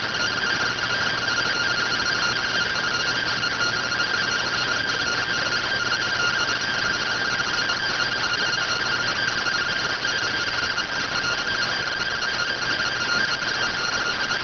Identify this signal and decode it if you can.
Two-tone teleprinter signal, about 170 Hz shift.